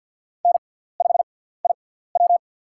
Morse code, heard as a keyed tone, about 48 wpm, spaced as for 17 wpm, 690 Hz. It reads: N5IW
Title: USA Call Signs - 4 Characters - 6x Character Spacing - 50wpm